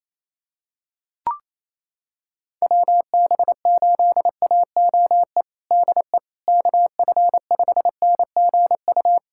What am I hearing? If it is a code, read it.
Morse code, heard as a keyed tone, about 27 wpm, 700 Hz. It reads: WB8AOE DE KF5NGU
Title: USA Callsign DE Callsign - 6 Characters - 28wpm